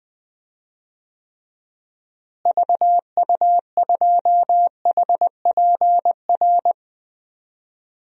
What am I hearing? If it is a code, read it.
Morse code, heard as a keyed tone, about 20 wpm, 700 Hz. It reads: VU2HPR
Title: DXCC-100 Call Signs - 6 Characters - Repeated 2x - 20wpm